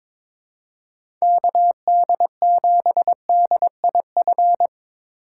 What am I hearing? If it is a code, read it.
Morse code, heard as a keyed tone, about 25 wpm, 700 Hz. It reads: KD7DIF
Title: Speed Racing - USA 2x3 Call Signs 22wpm